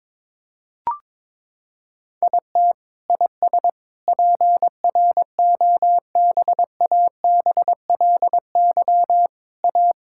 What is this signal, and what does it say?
Morse code, heard as a keyed tone, about 22 wpm, 700 Hz. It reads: IT IS PROBABLY A
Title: Sentences from Top 1000 Words - Review Part 6 - 22wpm